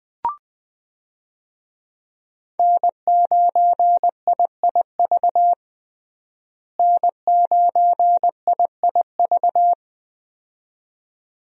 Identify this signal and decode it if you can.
Morse code, heard as a keyed tone, about 20 wpm, 700 Hz. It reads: N9IIV N9IIV
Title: USA Call Signs - 5 Characters - Repeated 2x - 20wpm